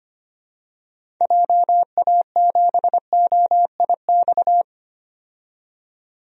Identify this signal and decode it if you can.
Morse code, heard as a keyed tone, about 25 wpm, 700 Hz. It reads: JA7OIX
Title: DXCC-100 Call Signs - 6 Characters - 25wpm